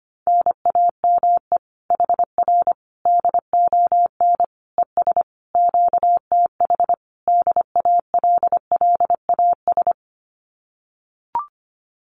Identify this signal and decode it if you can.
Morse code, heard as a keyed tone, about 25 wpm, 700 Hz. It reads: NAME HR DON ES QTH DALLAS
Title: Common QSO Phrases Encore 25wpm